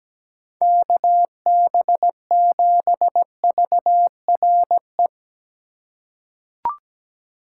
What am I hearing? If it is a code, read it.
Morse code, heard as a keyed tone, about 17 wpm, 700 Hz. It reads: KB7VRE
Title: USA Call Signs - 6 Characters - 4x Character Spacing - 17wpm